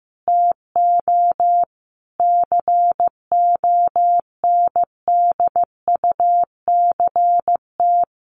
Morse code, heard as a keyed tone, about 15 wpm, 700 Hz. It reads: TO CONDUCT
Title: A Sound Of Thunder 15wpm